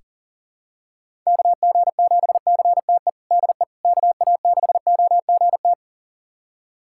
Morse code, heard as a keyed tone, about 40 wpm, 700 Hz. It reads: KG7CTE DE KA6OGT
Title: USA Callsign DE Callsign - 6 Characters - 40wpm